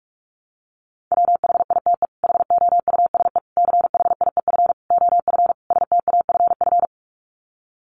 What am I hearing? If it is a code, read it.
Morse code, heard as a keyed tone, about 44 wpm, 700 Hz. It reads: WHITE HOUSE CHIEF OF STAFF
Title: Sets of 5 Words Encore 45wpm